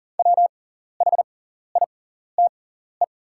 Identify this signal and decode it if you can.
Morse code, heard as a keyed tone, about 40 wpm, spaced as for 14 wpm, 700 Hz. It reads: WHITE
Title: 6x Character Spacing - Top 300 Words - 40wpm